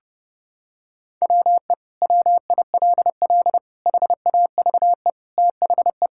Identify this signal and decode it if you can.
Morse code, heard as a keyed tone, about 30 wpm, 700 Hz. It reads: WE WILL HAVE THE
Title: Sentences from Top 1000 Words - Review Part 6 - 30wpm